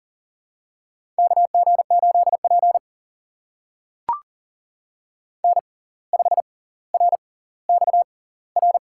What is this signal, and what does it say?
Morse code, heard as a keyed tone, about 40 wpm, spaced as for 19 wpm, 700 Hz. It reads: KG8P N5RXR
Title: USA Call Signs - 6x Character Spacing - 40wpm